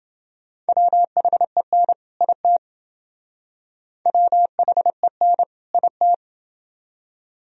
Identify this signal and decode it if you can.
Morse code, heard as a keyed tone, about 28 wpm, 700 Hz. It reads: WHEN IT WHEN IT